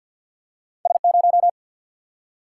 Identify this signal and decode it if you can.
Morse code, heard as a keyed tone, about 53 wpm, 700 Hz. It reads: S0
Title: Single-Letter Single-Number 50wpm